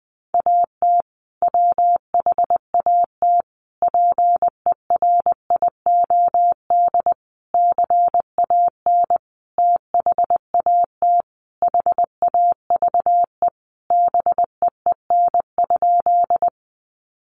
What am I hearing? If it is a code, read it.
Morse code, heard as a keyed tone, about 20 wpm, 700 Hz. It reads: AT WHAT PERIOD CAN THAT HAVE BEEN?
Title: Sentences from Top 800 Words 20wpm